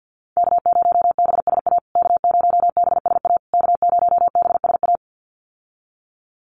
Morse code, heard as a keyed tone, about 48 wpm, 700 Hz. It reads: K0BSA K0BSA K0BSA
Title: USA Call Signs Repeated 3x 50wpm